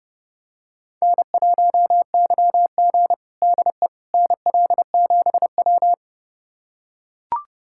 Morse code, heard as a keyed tone, about 30 wpm, 700 Hz. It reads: N1YG DE NL7W
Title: USA Callsign DE Callsign - 4 Characters - 30wpm